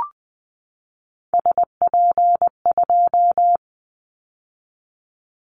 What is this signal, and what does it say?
Morse code, heard as a keyed tone, about 20 wpm, 700 Hz. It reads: SP2
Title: Callsign Prefixes for Top 25 DXCC - 20wpm